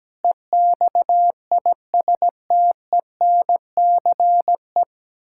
Morse code, heard as a keyed tone, about 17 wpm, 700 Hz. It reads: EXISTENCE
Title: Pangram Challenge v2 - 17wpm